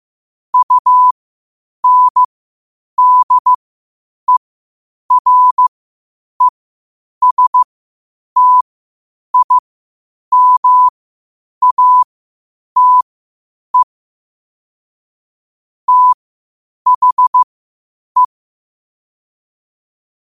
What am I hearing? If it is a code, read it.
Morse code, heard as a keyed tone, about 15 wpm, spaced as for 8 wpm, 1000 Hz. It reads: UNDERESTIMATE THE
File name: qotdfiles / 8wpm